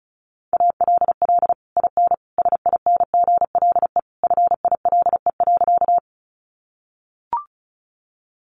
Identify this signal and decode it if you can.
Morse code, heard as a keyed tone, about 34 wpm, 700 Hz. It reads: ALL IN SINGLE FILE.
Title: Sentences from Top 1200 Words 35wpm